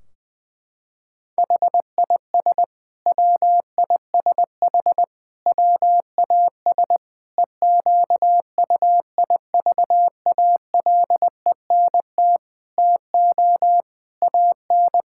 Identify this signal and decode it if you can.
Morse code, heard as a keyed tone, about 20 wpm, 700 Hz. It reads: HIS WISH WAS EQUIVALENT TO AN
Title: Sentences from Top 1600 Words 20wpm